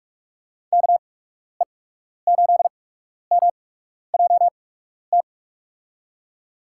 Morse code, heard as a keyed tone, about 46 wpm, spaced as for 12 wpm, 700 Hz. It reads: KE8MJT
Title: USA Call Signs - 8x Character Spacing - 45wpm